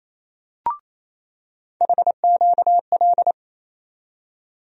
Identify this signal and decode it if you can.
Morse code, heard as a keyed tone, about 28 wpm, 700 Hz. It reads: HQL
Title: USA Call Sign Postfixes - 3 Letters 28wpm